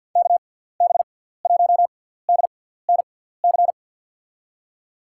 Morse code, heard as a keyed tone, about 47 wpm, 700 Hz. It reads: K B 1 D N C
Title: USA Call Signs - 6 Characters - 6x Character Spacing - 50wpm